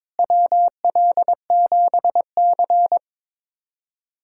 Morse code, heard as a keyed tone, about 22 wpm, 700 Hz. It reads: WL7C
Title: USA Callsign DE Callsign - 4 Characters - 22wpm